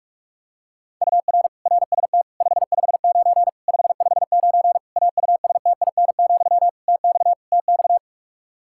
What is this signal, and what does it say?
Morse code, heard as a keyed tone, about 46 wpm, 700 Hz. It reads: UR RST 559 559 AUSTIN, TX TX
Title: K1USN SST QSO Practice 45wpm